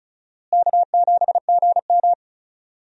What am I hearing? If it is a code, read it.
Morse code, heard as a keyed tone, about 34 wpm, 700 Hz. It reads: K7GM
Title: USA Call Signs - 4 Characters - Repeated 2x - 35wpm